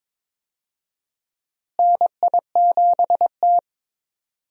Morse code, heard as a keyed tone, about 22 wpm, 700 Hz. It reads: NI7T
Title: USA Call Signs - 4 Characters - Repeated 3x - 22wpm